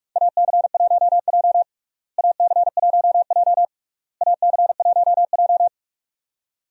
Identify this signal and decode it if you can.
Morse code, heard as a keyed tone, about 46 wpm, 700 Hz. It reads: AC1J AC1J AC1J
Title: USA Call Signs - 4 Characters - Repeated 3x - 45wpm